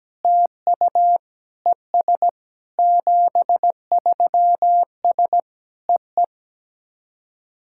Morse code, heard as a keyed tone, about 17 wpm, 700 Hz. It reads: TU ES 73S EE
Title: SOTA QSO Practice 17wpm